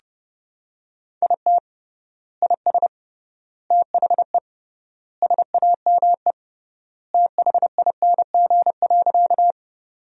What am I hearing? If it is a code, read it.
Morse code, heard as a keyed tone, about 30 wpm, 700 Hz. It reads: IT IS THE SAME THING.
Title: 3x Word-Spacing - Sentences from Top 200 Words - 30wpm